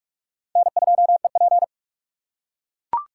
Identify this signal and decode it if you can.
Morse code, heard as a keyed tone, about 41 wpm, 700 Hz. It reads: N2EP